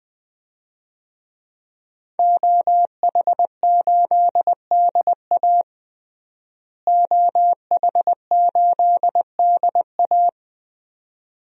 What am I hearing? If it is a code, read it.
Morse code, heard as a keyed tone, about 20 wpm, 700 Hz. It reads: OH8DA OH8DA
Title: DXCC-100 Call Signs - 5 Characters - Repeated 2x - 20wpm